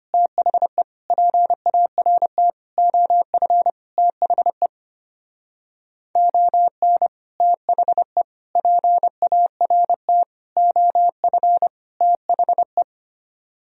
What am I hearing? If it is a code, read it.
Morse code, heard as a keyed tone, about 28 wpm, 700 Hz. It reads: THE PART OF THE ON THE PART OF THE
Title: Speed-Racing - Sets of 5 Words 25wpm